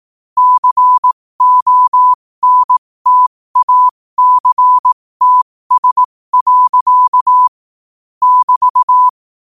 Morse code, heard as a keyed tone, about 18 wpm, spaced as for 15 wpm, 1000 Hz. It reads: CONTACTS. <BT>
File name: qotdfiles / 15wpm